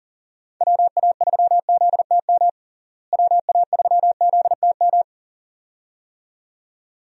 Morse code, heard as a keyed tone, about 40 wpm, 700 Hz. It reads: WA3ZTM WA3ZTM
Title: USA Call Signs - 6 Characters - Repeated 2x - 40wpm